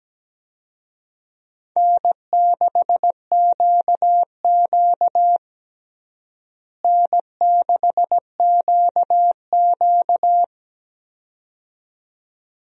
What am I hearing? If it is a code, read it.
Morse code, heard as a keyed tone, about 17 wpm, 700 Hz. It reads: N6QQ N6QQ
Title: USA Call Signs - 4 Characters - Repeated 2x - 17wpm